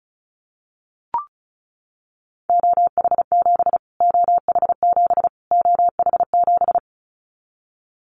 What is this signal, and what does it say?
Morse code, heard as a keyed tone, about 34 wpm, 700 Hz. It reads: OH7 OH7 OH7
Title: Callsign Prefixes for Top 25 DXCC - Repeated 3x - 35wpm